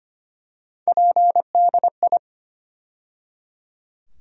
Morse code, heard as a keyed tone, about 25 wpm, 700 Hz. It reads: PDI
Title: USA Call Sign Postfixes - 3 Letters 25wpm